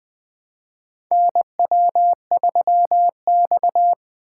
Morse code, heard as a keyed tone, about 20 wpm, 700 Hz. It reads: NW3X
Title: USA Call Signs - 4 Characters - 3x Character Spacing - 20wpm